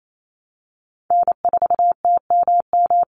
Morse code, heard as a keyed tone, about 28 wpm, 700 Hz. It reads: N4TMM
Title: Call Signs Repeated 3x - 28wpm